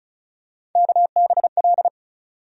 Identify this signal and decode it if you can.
Morse code, heard as a keyed tone, about 36 wpm, 700 Hz. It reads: KBL
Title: USA Call Sign Postfixes - 3 Letters 35wpm